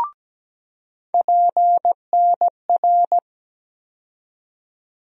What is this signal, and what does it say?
Morse code, heard as a keyed tone, about 17 wpm, 700 Hz. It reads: PNR
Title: USA Call Sign Postfixes - 3 Letters 17wpm